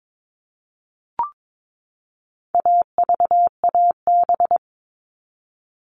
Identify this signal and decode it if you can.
Morse code, heard as a keyed tone, about 22 wpm, 700 Hz. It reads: AVAB
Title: All Cut Numbers - 4-Digit 22wpm